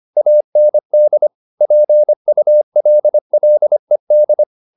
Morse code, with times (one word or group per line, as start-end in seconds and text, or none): 0.2-1.3 AND
1.6-4.4 PULLED